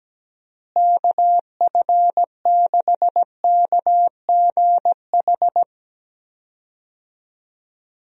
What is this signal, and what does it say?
Morse code, heard as a keyed tone, about 17 wpm, 700 Hz. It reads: KF6KGH
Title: USA Call Signs - 6 Characters - Repeated 3x - 17wpm